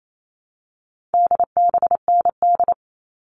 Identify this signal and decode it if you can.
Morse code, heard as a keyed tone, about 28 wpm, 700 Hz. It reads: DBND
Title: All Cut Numbers - 4-Digit 28wpm